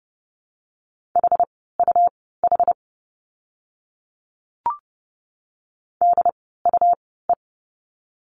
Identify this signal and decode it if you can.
Morse code, heard as a keyed tone, about 30 wpm, spaced as for 17 wpm, 700 Hz. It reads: HUH DUE
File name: Lesson 17f - Letter u - Words 30wpm